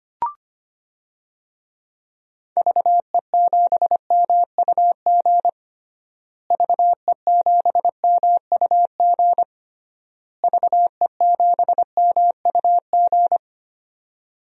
Morse code, heard as a keyed tone, about 25 wpm, 700 Hz. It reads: VE7MUG VE7MUG VE7MUG
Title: DXCC-100 Call Signs - 6 Characters - Repeated 3x - 25wpm